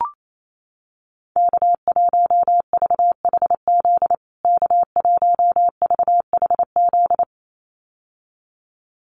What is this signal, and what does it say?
Morse code, heard as a keyed tone, about 28 wpm, 700 Hz. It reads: K1VHZ K1VHZ
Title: USA Call Signs Repeated 2x 28wpm